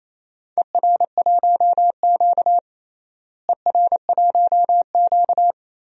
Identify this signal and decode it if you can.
Morse code, heard as a keyed tone, about 28 wpm, 700 Hz. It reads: ER1Q ER1Q
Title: DXCC-100 Call Signs - 4 Characters - Repeated 2x - 28wpm